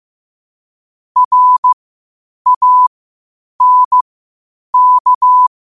Morse code, heard as a keyed tone, about 15 wpm, spaced as for 8 wpm, 1000 Hz. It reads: RANK
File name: qotdfiles / 8wpm